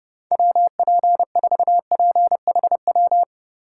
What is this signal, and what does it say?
Morse code, heard as a keyed tone, about 30 wpm, 700 Hz. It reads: WP4PHW
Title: USA Call Signs - 6 Characters - 4x Character Spacing - 30wpm